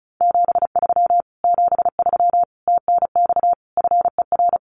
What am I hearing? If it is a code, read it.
Morse code, heard as a keyed tone, about 34 wpm, 700 Hz. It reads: 73 73 TNX FER
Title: Common QSO Phrases Encore 35wpm